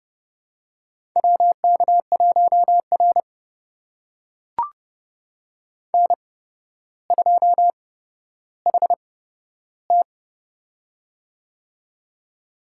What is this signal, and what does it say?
Morse code, heard as a keyed tone, about 31 wpm, spaced as for 13 wpm, 700 Hz. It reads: WK1R N2HT
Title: USA Call Signs - 4 Characters - 8x Character Spacing - 30wpm